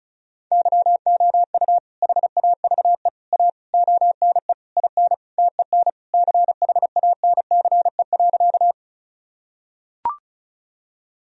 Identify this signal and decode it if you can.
Morse code, heard as a keyed tone, about 36 wpm, 700 Hz. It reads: YOU HAVE A ONE IN TEN CHANCE.
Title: Sentences from Top 1000 Words - Review Part 2 - 35wpm